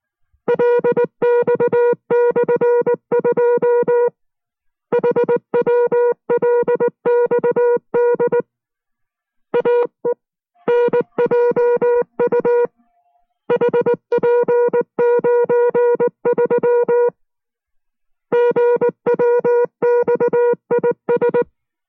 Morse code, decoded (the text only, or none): LX/2 HWLXD AENJU HP93 GWXIS